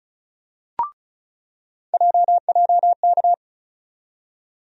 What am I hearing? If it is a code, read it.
Morse code, heard as a keyed tone, about 36 wpm, 700 Hz. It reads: JJK
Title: USA Call Sign Postfixes - 3 Letters 35wpm